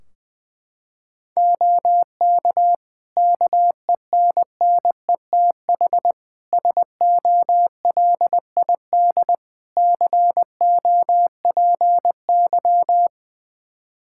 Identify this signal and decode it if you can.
Morse code, heard as a keyed tone, about 20 wpm, 700 Hz. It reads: OK KENNETH SOLID COPY
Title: Practice QSO - Exchange 3 - More Info - 20wpm